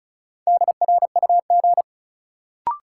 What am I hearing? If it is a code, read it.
Morse code, heard as a keyed tone, about 36 wpm, 700 Hz. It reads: DRUG
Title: Sets of 5 Words - New Word Drill - 35wpm